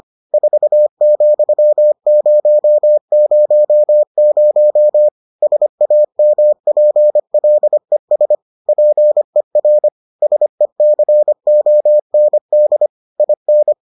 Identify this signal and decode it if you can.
Morse code, heard as a keyed tone, about 25 wpm, 600 Hz. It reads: 4,000 SAMPLES PER SECOND IN